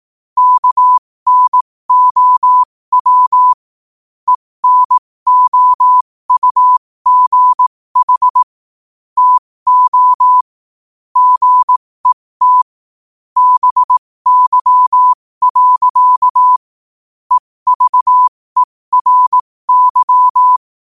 Morse code, heard as a keyed tone, about 18 wpm, spaced as for 15 wpm, 1000 Hz. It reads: KNOW ENOUGH TO GET BY. EVERY
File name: qotdfiles / 15wpm